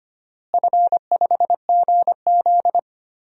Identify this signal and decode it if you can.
Morse code, heard as a keyed tone, about 25 wpm, 700 Hz. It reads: F5GZ
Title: DXCC-100 Call Signs - 4 Characters - Repeated 3x - 25wpm